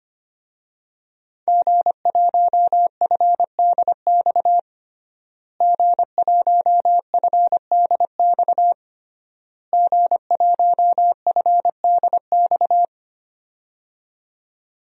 Morse code, heard as a keyed tone, about 25 wpm, 700 Hz. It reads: G1FDX G1FDX G1FDX
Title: DXCC-100 Call Signs - 5 Characters - Repeated 3x - 25wpm